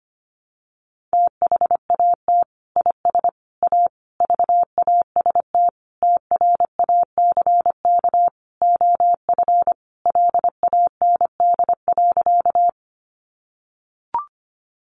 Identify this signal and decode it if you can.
Morse code, heard as a keyed tone, about 25 wpm, 700 Hz. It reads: THAT IS A VAST TRACK OF LAND.
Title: Sentences from Top 1300 Words 25wpm